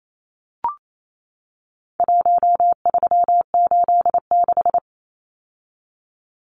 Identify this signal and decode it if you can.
Morse code, heard as a keyed tone, about 28 wpm, 700 Hz. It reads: 1386
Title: Numbers - Contest Style - 4-Digit 28wpm